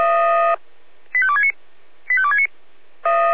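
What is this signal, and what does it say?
Five-tone selective call (ZVEI), heard as quick runs of five tones, tones of about 70 ms. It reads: 85269 85269
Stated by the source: ZVEI five-tone selective call 85330, received via SDR.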